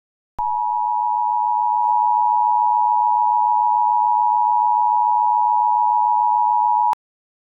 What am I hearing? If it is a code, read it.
Emergency Alert System (SAME) tones (the 853 and 960 Hz attention tone). It is heard from 0.3 to 7.0 s.